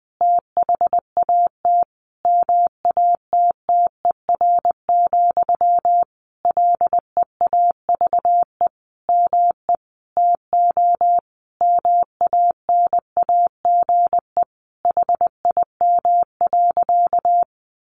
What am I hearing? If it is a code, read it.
Morse code, heard as a keyed tone, about 20 wpm, 700 Hz. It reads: THAT MATTER, LEAVE ME TO MANAGE HIM.
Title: Sentences from Top 1400 Words 20wpm